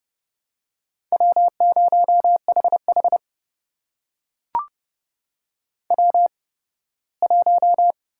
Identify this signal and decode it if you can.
Morse code, heard as a keyed tone, about 30 wpm, spaced as for 18 wpm, 700 Hz. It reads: W0HH W1
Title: USA Call Signs - 4 Characters - 8x Character Spacing - 30wpm